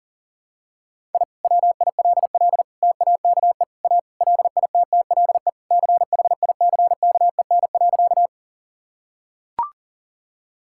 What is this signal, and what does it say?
Morse code, heard as a keyed tone, about 40 wpm, 700 Hz. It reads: I WILL TAKE A LITTLE CHICKEN.
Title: Sentences from Top 1000 Words - Review Part 5 - 40wpm